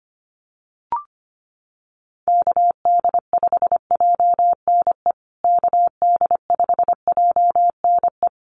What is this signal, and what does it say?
Morse code, heard as a keyed tone, about 25 wpm, 700 Hz. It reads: KD5JNE KD5JNE
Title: USA Call Signs Repeated 3x 25wpm